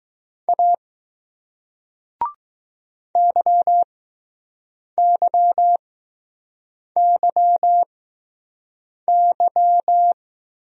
Morse code, heard as a keyed tone, about 20 wpm, spaced as for 6 wpm, 700 Hz. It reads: A YYYY